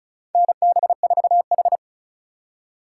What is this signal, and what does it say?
Morse code, heard as a keyed tone, about 36 wpm, 700 Hz. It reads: NB4H
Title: USA Call Signs - 4 Characters - 2x Character Spacing - 35wpm